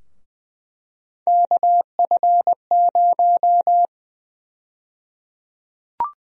Morse code, heard as a keyed tone, about 20 wpm, 700 Hz. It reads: KF0
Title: USA Call Sign Prefixes - 20wpm